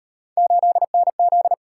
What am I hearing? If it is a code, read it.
Morse code, heard as a keyed tone, about 36 wpm, 700 Hz. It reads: 8NZ